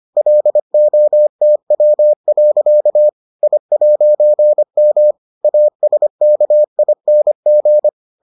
Morse code, heard as a keyed tone, about 25 wpm, 600 Hz. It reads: LOTW. I'M ASKING